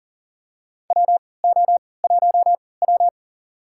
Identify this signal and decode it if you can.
Morse code, heard as a keyed tone, about 40 wpm, spaced as for 23 wpm, 700 Hz. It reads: WO1W
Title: USA Call Signs - 4 Characters - 3x Character Spacing - 40wpm